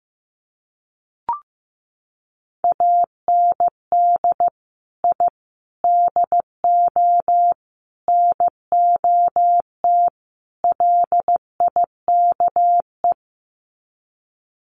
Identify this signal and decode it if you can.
Morse code, heard as a keyed tone, about 15 wpm, 700 Hz. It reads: AND I DO NOT LIKE
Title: Sets of 5 Words Encore2 15wpm